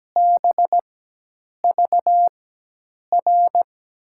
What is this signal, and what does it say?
Morse code, heard as a keyed tone, about 17 wpm, spaced as for 8 wpm, 700 Hz. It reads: BVR
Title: USA Call Signs - 6 Characters - 4x Character Spacing - 17wpm